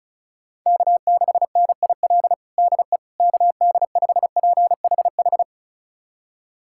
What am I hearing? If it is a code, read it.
Morse code, heard as a keyed tone, about 36 wpm, 690 Hz. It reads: K6NIL DE KD5PHH
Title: USA Callsign DE Callsign 35wpm